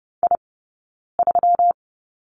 Morse code, heard as a keyed tone, about 30 wpm, 700 Hz. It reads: I 3
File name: Lesson 20d - Number 3 - Two Characters 30wpm